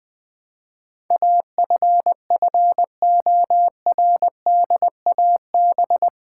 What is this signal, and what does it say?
Morse code, heard as a keyed tone, about 20 wpm, 700 Hz. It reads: AFFORDAB
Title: Build A Word - 4 Letters At A Time - 10-Letter Words - 20wpm